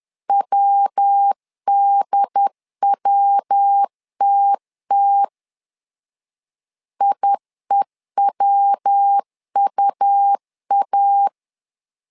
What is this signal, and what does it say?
Morse code, heard as a keyed tone, about 10 wpm, 800 Hz. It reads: WDWTT IEWUA
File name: LEZIONI / 3 MWRO / 3 MW 10wpm 1959